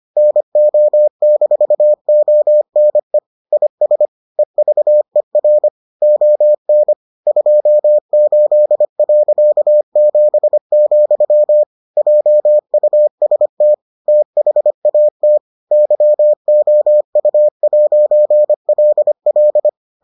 Morse code, heard as a keyed tone, about 25 wpm, 600 Hz. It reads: NO-ONE IS EVER ON 28.7, JUST THAT YOU'LL